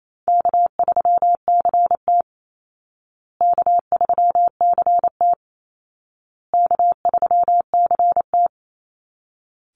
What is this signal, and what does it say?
Morse code, heard as a keyed tone, about 28 wpm, 700 Hz. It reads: K3CT K3CT K3CT
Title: Easy Call Signs Repeated 3x - 28wpm